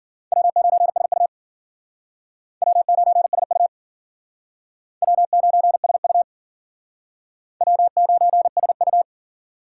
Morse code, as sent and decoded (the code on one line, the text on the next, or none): .-- ----. ... ..- / .-- ----. ... ..- / .-- ----. ... ..- / .-- ----. ... ..-
W9SU W9SU W9SU W9SU